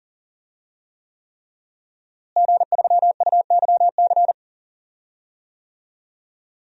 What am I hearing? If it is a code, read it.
Morse code, heard as a keyed tone, about 40 wpm, 700 Hz. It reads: G3UYC
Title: DXCC-100 Call Signs - 5 Characters - Repeated 3x - 40wpm